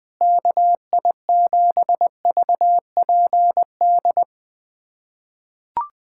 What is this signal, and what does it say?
Morse code, heard as a keyed tone, about 20 wpm, 700 Hz. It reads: KI7VPD
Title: USA Callsign DE Callsign 20wpm